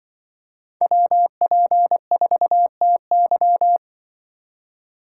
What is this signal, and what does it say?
Morse code, heard as a keyed tone, about 23 wpm, 700 Hz. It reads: WP4TY
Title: Speed Racing - USA 2x2 Call Signs 20wpm